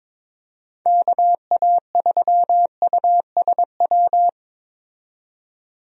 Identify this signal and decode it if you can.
Morse code, heard as a keyed tone, about 22 wpm, 700 Hz. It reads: KA3USW
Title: USA Call Signs - 6 Characters - 22wpm